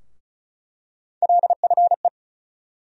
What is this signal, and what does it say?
Morse code, heard as a keyed tone, about 34 wpm, 700 Hz. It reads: LFE
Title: USA Call Sign Postfixes - 3 Letters 35wpm